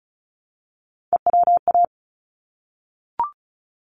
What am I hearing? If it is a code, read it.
Morse code, heard as a keyed tone, about 34 wpm, 700 Hz. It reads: EWA
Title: ARRL Sections 35wpm